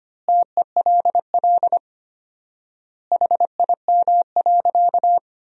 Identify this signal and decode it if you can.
Morse code, heard as a keyed tone, about 25 wpm, 700 Hz. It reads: TELL HIM.
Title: 4x Word-Spacing - Sentences from Top 200 Words - 25wpm